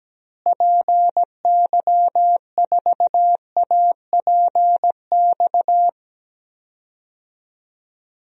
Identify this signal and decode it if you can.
Morse code, heard as a keyed tone, about 17 wpm, 700 Hz. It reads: PY4APX
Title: DXCC-100 Call Signs - 6 Characters - Repeated 3x - 17wpm